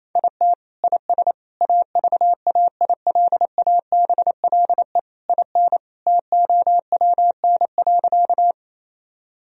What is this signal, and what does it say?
Morse code, heard as a keyed tone, about 28 wpm, 700 Hz. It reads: IT IS AVAILABLE IN TOWN.
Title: Sentences from Top 1000 Words - Review Part 3 - 28wpm